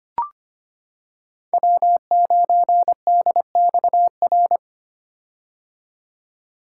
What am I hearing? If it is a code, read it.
Morse code, heard as a keyed tone, about 25 wpm, 700 Hz. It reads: W9DXR
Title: K1USN SST QSO Practice - 25wpm